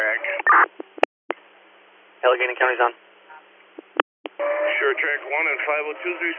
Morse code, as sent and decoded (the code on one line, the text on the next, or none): none